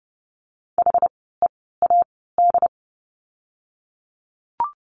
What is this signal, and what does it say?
Morse code, heard as a keyed tone, about 30 wpm, spaced as for 17 wpm, 700 Hz. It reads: HEAD
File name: Lesson 16f - Letter c - Words 30wpm